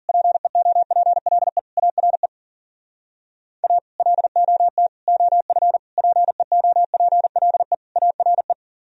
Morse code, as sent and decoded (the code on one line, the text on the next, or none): .--. --- .--. .-.. / .- .-. / .- / .-.. --- - / --- ..-. / .--. --- .--. .-.. / .- .-. .
PEOPLE ARE A LOT OF PEOPLE ARE